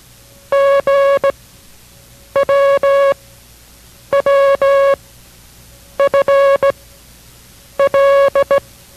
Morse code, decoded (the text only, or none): G W W F L